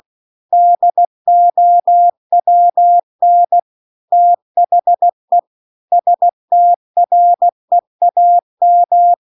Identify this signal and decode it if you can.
Morse code, heard as a keyed tone, about 16 wpm, 700 Hz. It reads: DOWN THE STREAM